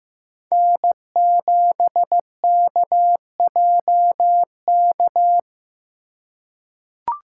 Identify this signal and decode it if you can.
Morse code, heard as a keyed tone, about 15 wpm, 700 Hz. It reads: N7KJK
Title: USA Call Signs - 5 Characters - 4x Character Spacing - 15wpm